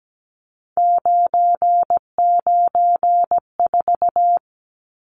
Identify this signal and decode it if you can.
Morse code, heard as a keyed tone, about 17 wpm, 700 Hz. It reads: 994
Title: Numbers - 3-Digit 17wpm